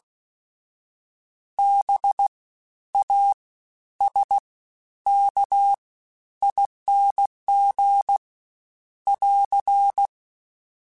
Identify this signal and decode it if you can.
Morse code, heard as a keyed tone, about 16 wpm, spaced as for 11 wpm, 800 Hz. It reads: BASKING <AR>
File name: Special INGs-mp3s / Special INGs-mp3s / Fast ing @15